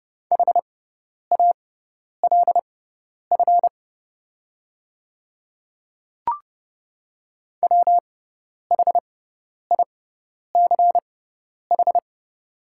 Morse code, heard as a keyed tone, about 30 wpm, spaced as for 10 wpm, 700 Hz. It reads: HALF WHICH
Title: 6x Character Spacing - Top 500 Words - 30wpm